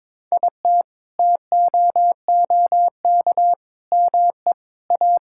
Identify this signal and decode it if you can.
Morse code, heard as a keyed tone, about 22 wpm, 700 Hz. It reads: IT TOOK ME A
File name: practice / 22wpm / Sentences with 8-Word Phrases - 22wpm